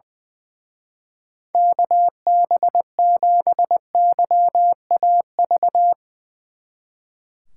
Morse code, heard as a keyed tone, about 22 wpm, 700 Hz. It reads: KB7YAV
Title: Speed Racing - USA 2x3 Call Signs 20wpm